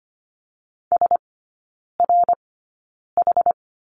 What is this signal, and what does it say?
Morse code, heard as a keyed tone, about 25 wpm, spaced as for 9 wpm, 690 Hz. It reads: SRH
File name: Lesson 14e - Number 2 - Three Characters 25wpm